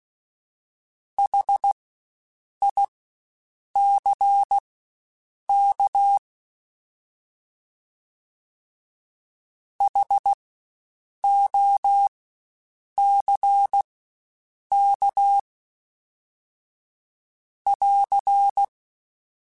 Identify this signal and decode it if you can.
Morse code, heard as a keyed tone, about 16 wpm, spaced as for 7 wpm, 800 Hz. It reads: HICK HOCK <AR>
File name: Spaced Alphas / Spaced Alphas mp3s / Hs sp @15